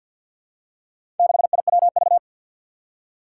Morse code, heard as a keyed tone, about 47 wpm, 700 Hz. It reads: 6IWV
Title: Four Challenging Letter-Number Combinations 50wpm